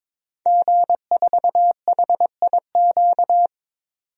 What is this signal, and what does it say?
Morse code, heard as a keyed tone, about 22 wpm, 700 Hz. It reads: G4HIQ
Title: DXCC-100 Call Signs - 5 Characters - 22wpm